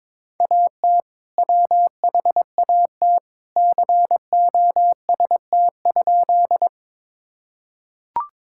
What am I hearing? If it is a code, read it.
Morse code, heard as a keyed tone, about 21 wpm, 700 Hz. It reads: AT WHAT COST?
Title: Sentences from Top 1000 Words - Review Part 5 - 22wpm